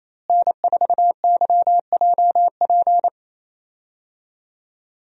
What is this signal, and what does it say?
Morse code, heard as a keyed tone, about 28 wpm, 700 Hz. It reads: N4YJP
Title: USA Call Signs - 5 Characters - Repeated 3x - 28wpm